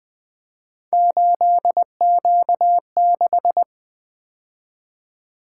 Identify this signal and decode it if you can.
Morse code, heard as a keyed tone, about 25 wpm, 700 Hz. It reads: 8Q6
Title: Speed-Racing - Three Letter-Number Combinations - 20wpm